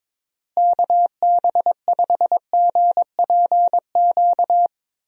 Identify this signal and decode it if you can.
Morse code, heard as a keyed tone, about 22 wpm, 700 Hz. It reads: KB5GPQ
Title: USA Call Signs - 6 Characters - Repeated 3x - 22wpm